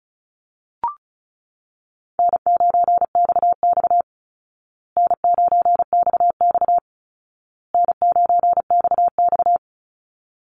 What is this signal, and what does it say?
Morse code, heard as a keyed tone, about 34 wpm, 700 Hz. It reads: N9XX N9XX N9XX
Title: Easy Call Signs Repeated 3x - 35wpm